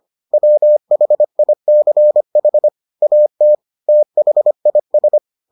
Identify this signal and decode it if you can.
Morse code, heard as a keyed tone, about 25 wpm, 600 Hz. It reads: WHICH AT THIS